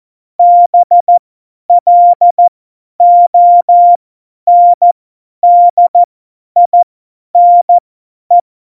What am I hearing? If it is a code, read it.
Morse code, heard as a keyed tone, about 14 wpm, spaced as for 10 wpm, 700 Hz. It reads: BLONDINE
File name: Morsekurs / Witze / Witz 04 Juli 17, Tempo50 F70